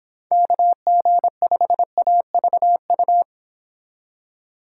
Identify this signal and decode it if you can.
Morse code, heard as a keyed tone, about 26 wpm, 700 Hz. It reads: KG5AVU